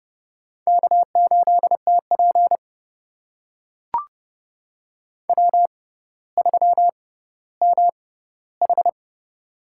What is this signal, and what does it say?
Morse code, heard as a keyed tone, about 30 wpm, spaced as for 15 wpm, 700 Hz. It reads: K8TP W3MH
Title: USA Call Signs - 4 Characters - 6x Character Spacing - 30wpm